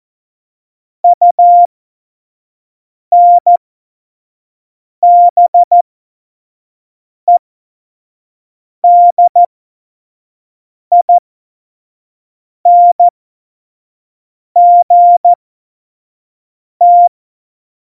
Morse code, heard as a keyed tone, about 14 wpm, spaced as for 5 wpm, 700 Hz. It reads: UNBEDINGT
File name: Morsekurs / Witze / Witz 09. Januar 2018, 25 F70